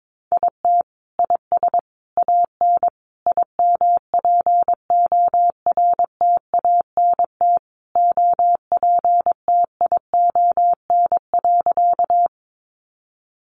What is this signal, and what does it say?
Morse code, heard as a keyed tone, about 22 wpm, 700 Hz. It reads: IT IS AN IMPORTANT OPTION.
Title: Sentences from Top 1300 Words 22wpm